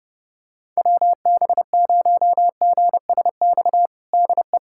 Morse code, heard as a keyed tone, about 30 wpm, 700 Hz. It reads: WB0GSX DE
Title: USA Callsign DE Callsign - 6 Characters - 30wpm